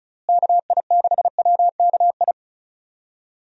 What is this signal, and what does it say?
Morse code, heard as a keyed tone, about 36 wpm, 700 Hz. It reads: KI6WKI
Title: USA Call Signs - 6 Characters - Repeated 3x - 35wpm